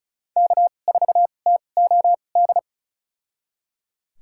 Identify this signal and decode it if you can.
Morse code, heard as a keyed tone, about 36 wpm, spaced as for 25 wpm, 700 Hz. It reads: K4TOD
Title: USA Call Signs - 5 Characters - 2x Character Spacing - 35wpm